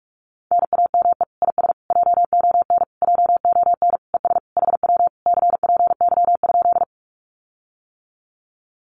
Morse code, heard as a keyed tone, about 46 wpm, 700 Hz. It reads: NAME IS JON JON ES HW CPY?
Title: Common QSO Phrases Encore 45wpm